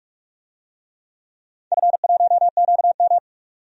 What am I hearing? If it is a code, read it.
Morse code, heard as a keyed tone, about 46 wpm, 700 Hz. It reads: F1QM
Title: DXCC-100 Call Signs - 4 Characters - 45wpm